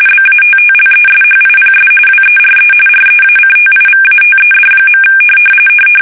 RTTY, about 45.45 baud. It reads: BATTLESHIPS WERE REACTIVATED.
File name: fsk850 / battleshipnj